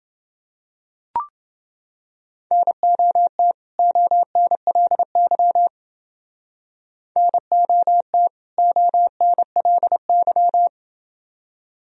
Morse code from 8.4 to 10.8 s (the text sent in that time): ONLY